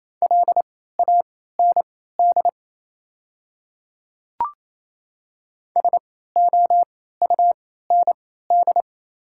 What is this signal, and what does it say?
Morse code, heard as a keyed tone, about 29 wpm, spaced as for 16 wpm, 690 Hz. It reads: LAND SOUND
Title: 3x Character Spacing - Top 200 Words - 28wpm